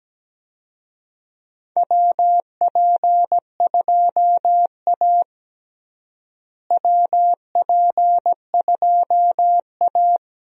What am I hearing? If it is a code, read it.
Morse code, heard as a keyed tone, about 17 wpm, 700 Hz. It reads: WP2A WP2A
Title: USA Call Signs - 4 Characters - Repeated 3x - 17wpm